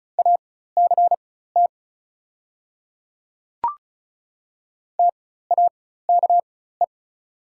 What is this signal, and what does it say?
Morse code, heard as a keyed tone, about 36 wpm, spaced as for 16 wpm, 700 Hz. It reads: ACT TAKE
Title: 4x Character Spacing - Top 200 Words - 35wpm